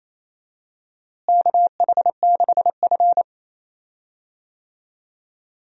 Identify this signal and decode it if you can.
Morse code, heard as a keyed tone, about 28 wpm, 700 Hz. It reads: KH6F
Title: USA Call Signs - 4 Characters - Repeated 2x - 28wpm